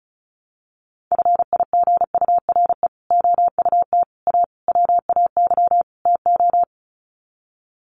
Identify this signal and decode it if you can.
Morse code, heard as a keyed tone, about 34 wpm, 700 Hz. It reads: FIGURE OUT A WAY TO
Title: Sets of 5 Words Encore2 35wpm